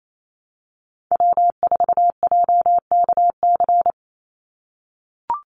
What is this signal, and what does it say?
Morse code, heard as a keyed tone, about 28 wpm, 700 Hz. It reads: W4JKC
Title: NA SOTA Chaser Callsigns 28wpm